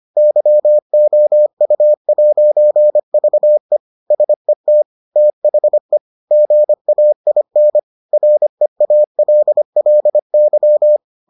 Morse code, heard as a keyed tone, about 25 wpm, 600 Hz. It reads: YOU'VE SET THE GAIN REALLY